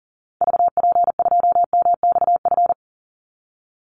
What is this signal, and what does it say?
Morse code, heard as a keyed tone, about 40 wpm, 700 Hz. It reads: VP2MXF
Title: Call Signs Repeated 3x - 40wpm